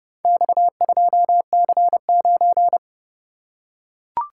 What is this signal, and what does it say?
Morse code, heard as a keyed tone, about 28 wpm, 700 Hz. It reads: X2C9